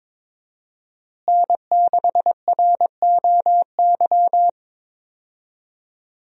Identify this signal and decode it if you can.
Morse code, heard as a keyed tone, about 22 wpm, 700 Hz. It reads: N6ROY
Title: USA Call Signs - 5 Characters - Repeated 3x - 22wpm